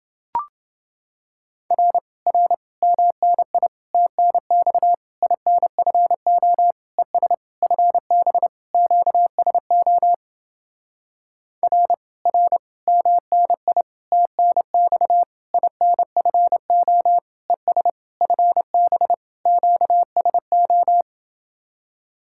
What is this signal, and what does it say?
Morse code, heard as a keyed tone, about 30 wpm, 700 Hz. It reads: R R MNI TNX INFO ES FB QSO R R MNI TNX INFO ES FB QSO